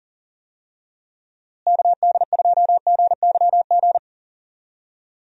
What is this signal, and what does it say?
Morse code, heard as a keyed tone, about 40 wpm, 700 Hz. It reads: KD2GYG
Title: USA Call Signs - 6 Characters - Repeated 2x - 40wpm